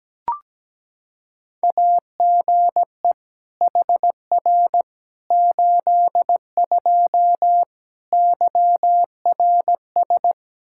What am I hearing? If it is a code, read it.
Morse code, heard as a keyed tone, about 17 wpm, 700 Hz. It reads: AGE HR 82 YRS
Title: K1USN SST QSO Practice 17wpm